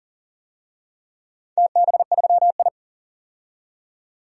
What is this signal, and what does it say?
Morse code, heard as a keyed tone, about 40 wpm, 700 Hz. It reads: TB3I
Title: DXCC-100 Call Signs - 4 Characters - 40wpm